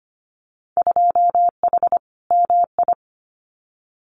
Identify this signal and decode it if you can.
Morse code, heard as a keyed tone, about 25 wpm, 700 Hz. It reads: 2H MI
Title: Winter Field Day Exchanges Encore 25wpm